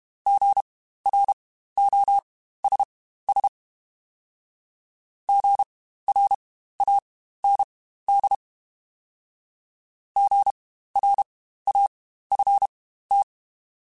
Morse code, heard as a keyed tone, about 32 wpm, spaced as for 15 wpm, 800 Hz. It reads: GROSS GRAND GRAFT
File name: Spaced Alphas / Spaced Alphas mp3s / Gs sp @30